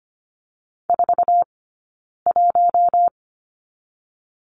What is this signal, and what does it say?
Morse code, heard as a keyed tone, about 25 wpm, 700 Hz. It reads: 4 1
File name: Lesson 15d - Number 5 - Two Characters 25wpm